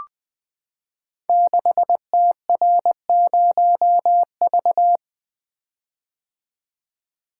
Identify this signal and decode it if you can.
Morse code, heard as a keyed tone, about 20 wpm, 700 Hz. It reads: 6TR0V
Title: Code Groups - 5 Letter:Numbers - 20wpm